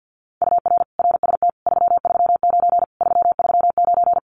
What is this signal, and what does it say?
Morse code, heard as a keyed tone, about 50 wpm, 700 Hz. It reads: UR RST 339 339
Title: Common QSO Phrases Encore 50wpm